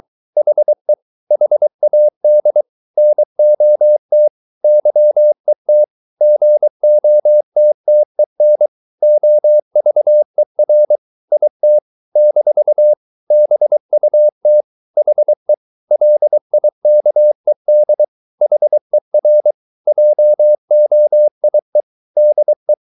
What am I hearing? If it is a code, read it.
Morse code, heard as a keyed tone, about 23 wpm, 600 Hz. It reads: HE HAD NOT YET GOTTEN OVER IT - BUT HE LIKED HER JOIE DE